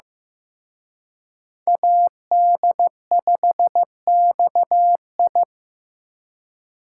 Morse code, heard as a keyed tone, about 18 wpm, 700 Hz. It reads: AD5XI
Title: Speed Racing - USA 2x2 Call Signs 15wpm